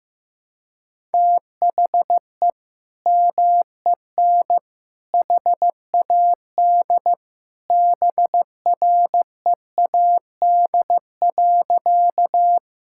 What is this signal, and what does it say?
Morse code, heard as a keyed tone, about 15 wpm, 700 Hz. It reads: THE MEN HAD BREAD.
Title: Sentences from Top 1600 Words 15wpm